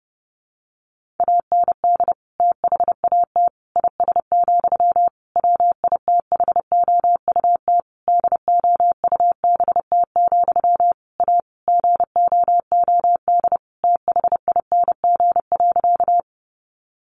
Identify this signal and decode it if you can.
Morse code, heard as a keyed tone, about 30 wpm, 700 Hz. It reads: AND THAT IS, WITHOUT DOUBT, A GOOD THING.
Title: Sentences from Top 1500 Words 30wpm